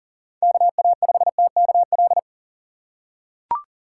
Morse code, heard as a keyed tone, about 40 wpm, 700 Hz. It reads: KA5TKL
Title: USA Call Signs - 6x Character Spacing - 40wpm